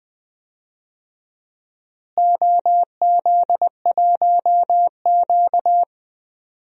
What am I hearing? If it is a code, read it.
Morse code, heard as a keyed tone, about 20 wpm, 700 Hz. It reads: OZ1Q
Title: DXCC-100 Call Signs - 4 Characters - Repeated 3x - 20wpm